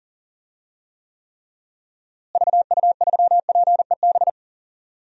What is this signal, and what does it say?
Morse code, heard as a keyed tone, about 40 wpm, 700 Hz. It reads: VU3PEB
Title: DXCC-100 Call Signs - 6 Characters - Repeated 3x - 40wpm